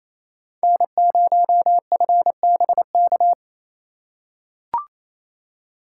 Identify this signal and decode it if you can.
Morse code, heard as a keyed tone, about 28 wpm, 700 Hz. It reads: N0FBK
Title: USA Call Signs - 5 Characters - 4x Character Spacing - 28wpm